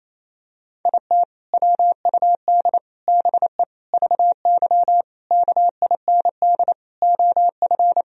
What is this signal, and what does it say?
Morse code, heard as a keyed tone, about 28 wpm, 700 Hz. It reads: IT WUD BE VY KIND OF
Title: Sentences from Top 200 Words and Abbreviations 28wpm